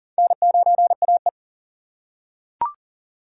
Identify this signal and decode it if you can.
Morse code, heard as a keyed tone, about 40 wpm, 700 Hz. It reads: N9AE
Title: USA Call Signs - 4 Characters - 4x Character Spacing - 40wpm